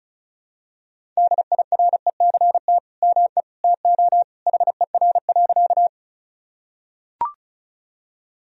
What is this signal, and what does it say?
Morse code, heard as a keyed tone, about 36 wpm, 700 Hz. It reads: DIRECT ME TO HER.
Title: Sentences from Top 1000 Words - Review Part 2 - 35wpm